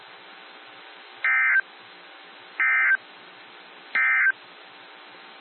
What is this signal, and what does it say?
Emergency Alert System (SAME) tones, heard as harsh buzzing bursts. It is heard from 1.1 to 4.4 s.
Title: NOAA Weather Radio - WXJ73